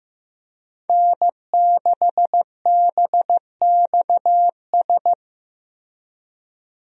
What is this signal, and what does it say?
Morse code, heard as a keyed tone, about 15 wpm, 700 Hz. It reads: N6BXS
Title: USA Call Signs - 5 Characters - Repeated 3x - 15wpm